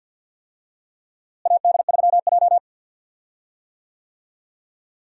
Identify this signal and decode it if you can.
Morse code, heard as a keyed tone, about 47 wpm, 700 Hz. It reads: AD3J
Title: USA Call Signs - 4 Characters - Repeated 3x - 50wpm